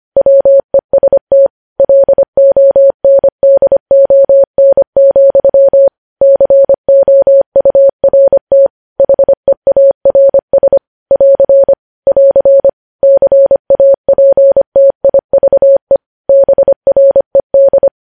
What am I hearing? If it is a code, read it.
Morse code, heard as a keyed tone, about 25 wpm, 550 Hz. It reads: WEST LONDON, COURT HEARS <AR> <AR> CAPTIVE BRED